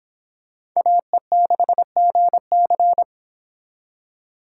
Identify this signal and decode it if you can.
Morse code, heard as a keyed tone, about 25 wpm, 700 Hz. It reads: AE6GC